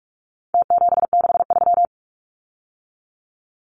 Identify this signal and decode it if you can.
Morse code, heard as a keyed tone, about 44 wpm, 700 Hz. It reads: T763
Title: Numbers - Contest Style - 4-Digit 45wpm